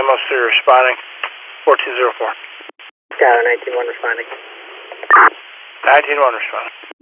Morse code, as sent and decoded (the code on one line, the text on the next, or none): none